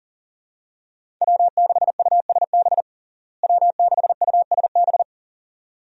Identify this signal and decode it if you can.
Morse code, heard as a keyed tone, about 40 wpm, 700 Hz. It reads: W6USB W6USB
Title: USA Call Signs - 5 Characters - Repeated 2x - 40wpm